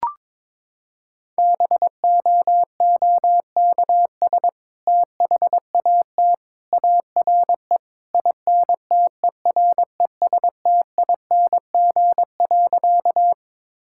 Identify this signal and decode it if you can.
Morse code, heard as a keyed tone, about 22 wpm, 700 Hz. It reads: BOOKS THAT ARE INTERESTING.
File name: practice / 22wpm / Sentences with 2-Word Phrases - 22wpm